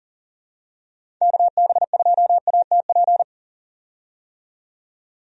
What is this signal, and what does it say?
Morse code, heard as a keyed tone, about 40 wpm, 700 Hz. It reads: KB2ATP
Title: USA Call Signs - 6 Characters - 40wpm